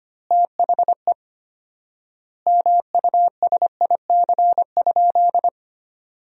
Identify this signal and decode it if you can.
Morse code, heard as a keyed tone, about 25 wpm, 700 Hz. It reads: THE MUSIC?
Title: 4x Word-Spacing - Sentences from Top 300 Words - 25wpm